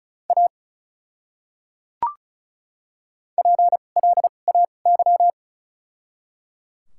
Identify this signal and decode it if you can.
Morse code, heard as a keyed tone, about 34 wpm, spaced as for 25 wpm, 700 Hz. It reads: A PLAY
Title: 2x Character Spacing - Top 200 Words - 35wpm